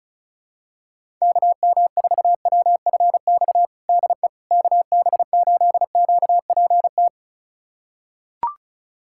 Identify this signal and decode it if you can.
Morse code, heard as a keyed tone, about 34 wpm, 700 Hz. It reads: KM4WFX DE KB8QPT
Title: USA Callsign DE Callsign - 6 Characters - 35wpm